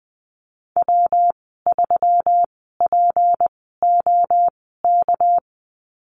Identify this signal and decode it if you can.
Morse code, heard as a keyed tone, about 20 wpm, spaced as for 15 wpm, 700 Hz. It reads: W3POK
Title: 207 Lesson 33g - Letter j - Call Signs 20wpm